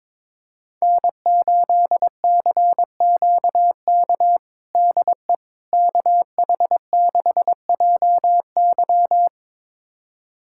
Morse code, heard as a keyed tone, about 22 wpm, 700 Hz. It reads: N8CQK DE KH6JY
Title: USA Callsign DE Callsign - 5 Characters - 22wpm